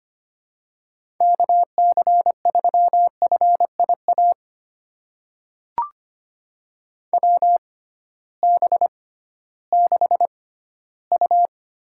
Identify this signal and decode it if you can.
Morse code, heard as a keyed tone, about 25 wpm, spaced as for 15 wpm, 700 Hz. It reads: KC3FIA WB6U
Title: USA Call Signs - 6 Characters - 6x Character Spacing - 25wpm